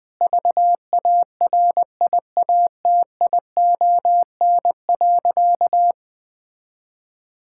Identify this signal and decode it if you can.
Morse code, heard as a keyed tone, about 20 wpm, 700 Hz. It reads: VARIATION.
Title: Sentences from Top 1600 Words 20wpm